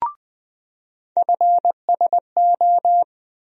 Morse code, heard as a keyed tone, about 20 wpm, 700 Hz. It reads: FSO
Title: USA Call Sign Postfixes - 3 Letters 20wpm